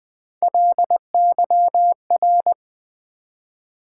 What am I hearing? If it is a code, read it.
Morse code, heard as a keyed tone, about 20 wpm, 700 Hz. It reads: LYR
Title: Build A Word - 1 Letter At A Time - 5-Letter Words - 20wpm